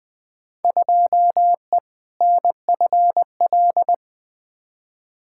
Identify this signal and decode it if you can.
Morse code, heard as a keyed tone, about 20 wpm, 700 Hz. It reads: 2E NFL
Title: Field Day Exchanges Encore - 20wpm